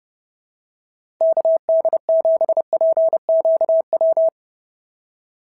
Morse code, heard as a keyed tone, about 26 wpm, 650 Hz. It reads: KD7PQW
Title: SOTA QSO Practice 30wpm